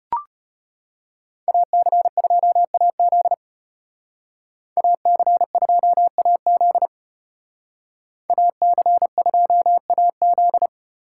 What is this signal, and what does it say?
Morse code, heard as a keyed tone, about 30 wpm, 700 Hz. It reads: AC2AZ AC2AZ AC2AZ
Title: Speed Racing - USA 2x2 Call Signs 25wpm